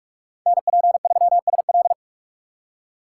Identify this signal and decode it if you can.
Morse code, heard as a keyed tone, about 46 wpm, 700 Hz. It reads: NP3SL
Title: USA Call Signs - 5 Characters - 4x Character Spacing - 45wpm